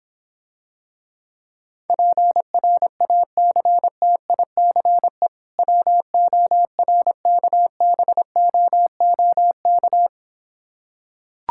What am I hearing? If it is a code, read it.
Morse code, heard as a keyed tone, about 26 wpm, 700 Hz. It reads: PRACTICE WORKBOOK